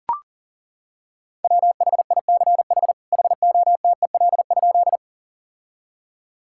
Morse code, heard as a keyed tone, about 40 wpm, 700 Hz. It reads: WHICH HOTEL?
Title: Sentences from Top 1600 Words 40wpm